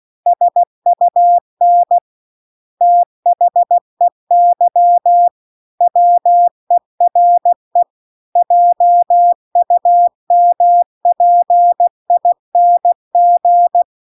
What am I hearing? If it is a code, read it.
Morse code, heard as a keyed tone, about 16 wpm, 700 Hz. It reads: SUN THEY WERE JUMPING
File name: CW-The-Big-Twoheaded-River-2-16WPM